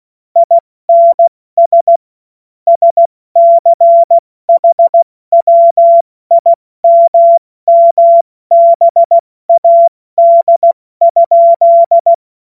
Morse code, heard as a keyed tone, about 16 wpm, spaced as for 14 wpm, 680 Hz. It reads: INS SCHWIMMBAD?
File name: Morsekurs / Witze / Witz 21. September 2021, 70 F80